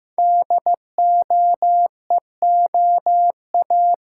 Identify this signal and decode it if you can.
Morse code, heard as a keyed tone, about 15 wpm, 700 Hz. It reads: DOEOA
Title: Code Groups - 5 Letters - 15wpm